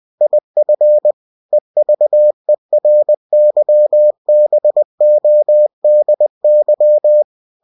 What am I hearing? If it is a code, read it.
Morse code, heard as a keyed tone, about 20 wpm, 600 Hz. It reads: IF EVERYBODY
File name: cw-books / winnie-the-pooh / winnie-20wpm-page-0048